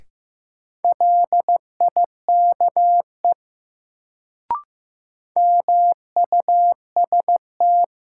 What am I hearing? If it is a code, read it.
Morse code, heard as a keyed tone, about 15 wpm, 700 Hz. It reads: LIKE MUST
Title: Sets of 2 Words - New Word Drill - 15wpm